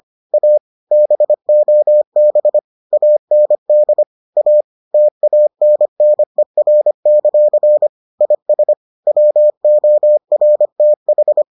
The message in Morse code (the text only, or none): A BOB AND A TANNER; IS WORTH